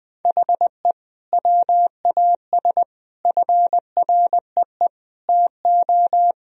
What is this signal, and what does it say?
Morse code, heard as a keyed tone, about 20 wpm, 700 Hz. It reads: HE WAS FREE TO
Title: Sentences from Top 1000 Words - Review Part 1 - 20wpm